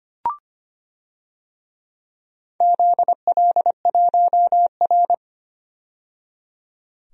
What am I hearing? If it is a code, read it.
Morse code, heard as a keyed tone, about 25 wpm, 700 Hz. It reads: ZL1R
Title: DXCC-100 Call Signs - 4 Characters - 25wpm